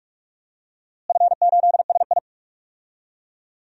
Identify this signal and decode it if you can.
Morse code, heard as a keyed tone, about 42 wpm, 700 Hz. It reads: F8SI
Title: DXCC-100 Call Signs - 4 Characters - Repeated 3x - 45wpm